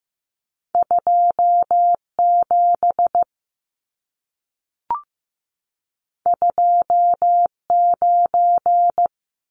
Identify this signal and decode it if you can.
Morse code, heard as a keyed tone, about 15 wpm, 700 Hz. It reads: 27 29
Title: Numbers - 2-Digit 15wpm